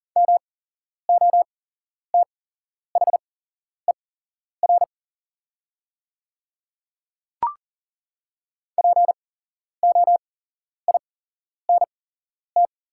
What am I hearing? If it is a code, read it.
Morse code, heard as a keyed tone, about 40 wpm, spaced as for 11 wpm, 700 Hz. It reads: MOTHER POINT
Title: 8x Character Spacing - Top 300 Words - 40wpm